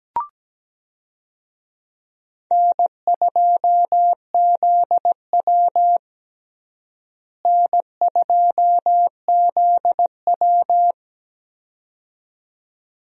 Morse code, heard as a keyed tone, about 17 wpm, 700 Hz. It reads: N2ZW N2ZW
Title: USA Call Signs - 4 Characters - Repeated 2x - 17wpm